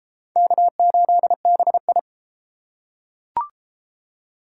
Morse code, heard as a keyed tone, about 34 wpm, 700 Hz. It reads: K8BI